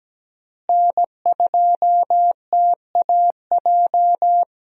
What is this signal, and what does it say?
Morse code, heard as a keyed tone, about 17 wpm, 700 Hz. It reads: N2TAJ
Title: USA Call Signs - 5 Characters - Repeated 2x - 17wpm